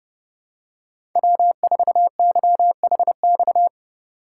Morse code, heard as a keyed tone, about 29 wpm, 700 Hz. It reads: W4YHX